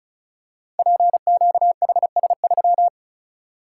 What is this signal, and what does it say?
Morse code, heard as a keyed tone, about 36 wpm, 700 Hz. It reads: PQHS3
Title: Code Groups - 5 Letter:Numbers - 35wpm